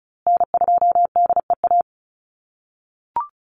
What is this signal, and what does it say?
Morse code, heard as a keyed tone, about 34 wpm, 700 Hz. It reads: N2DEA
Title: USA Call Signs Repeated 3x 35wpm